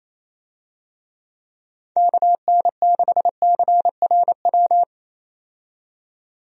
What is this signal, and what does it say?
Morse code, heard as a keyed tone, about 28 wpm, 700 Hz. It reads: KN6CRW
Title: USA Call Signs - 6 Characters - 28wpm